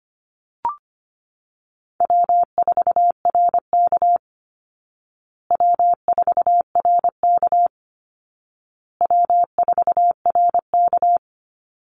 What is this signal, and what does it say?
Morse code, heard as a keyed tone, about 25 wpm, 700 Hz. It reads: W4RK W4RK W4RK
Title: Easy Call Signs Repeated 3x - 25wpm